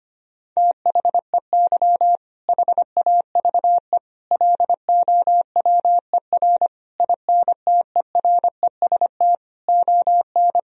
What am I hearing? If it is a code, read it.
Morse code, heard as a keyed tone, about 25 wpm, 700 Hz. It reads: THEY HAVE LOWER INTEREST ON
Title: Sentences from Top 1000 Words - Review Part 4 - 25wpm